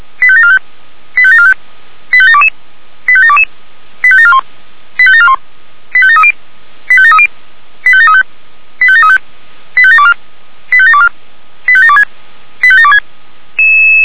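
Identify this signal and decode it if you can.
Five-tone selective call (ZVEI), heard as quick runs of five tones, tones of about 70 ms. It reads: 85745 85745 85722 85722 85741 85741 85730 85730 85735 85735 85724 85724 85726 85726
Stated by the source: ZVEI five-tone selective call received by SDR.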